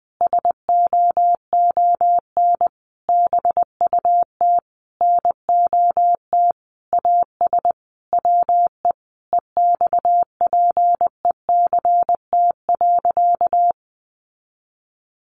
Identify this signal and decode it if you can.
Morse code, heard as a keyed tone, about 20 wpm, 700 Hz. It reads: SOON BUT NOT AS WE EXPECT.
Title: Sentences from Top 1100 Words 20wpm